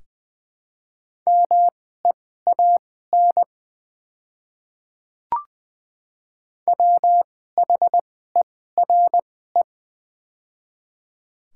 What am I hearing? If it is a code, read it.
Morse code, heard as a keyed tone, about 20 wpm, spaced as for 14 wpm, 700 Hz. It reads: MEAN WHERE
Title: 2x Character Spacing - Top 200 Words - 20wpm